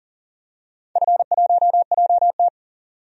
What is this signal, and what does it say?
Morse code, heard as a keyed tone, about 40 wpm, 700 Hz. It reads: F1JT
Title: DXCC-100 Call Signs - 4 Characters - 40wpm